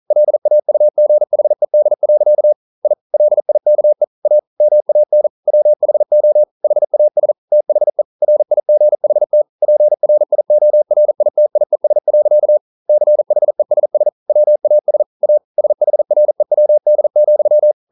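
Morse code, heard as a keyed tone, about 42 wpm, 600 Hz. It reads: LAUGHED. I LIKE A MAN WHO HAS THE RIGHT PRIORITIES. CHESS WAS A SHREWD,
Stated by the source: Morse code practice at 41 wpm, from CW med Gnister.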